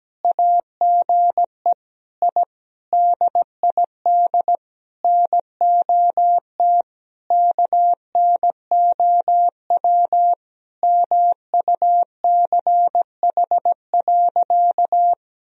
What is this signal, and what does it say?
Morse code, heard as a keyed tone, about 17 wpm, 700 Hz. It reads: AGE I DID NOT KNOW MUCH.
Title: Sentences from Top 1000 Words - Review Part 3 - 17wpm